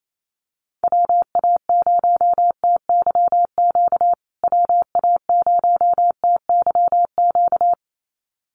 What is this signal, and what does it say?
Morse code, heard as a keyed tone, about 28 wpm, 700 Hz. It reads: WA0TYQ WA0TYQ
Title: USA Call Signs Repeated 2x 28wpm